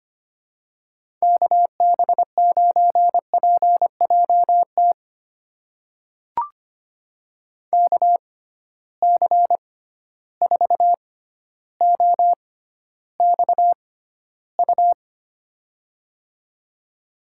Morse code, heard as a keyed tone, about 25 wpm, spaced as for 13 wpm, 700 Hz. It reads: KB9PJT KC4OXU
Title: USA Call Signs - 6 Characters - 6x Character Spacing - 25wpm